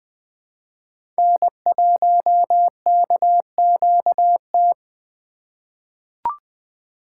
Morse code, heard as a keyed tone, about 20 wpm, 700 Hz. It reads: N1KQT
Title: USA Call Signs - 5 Characters - 8x Character Spacing - 20wpm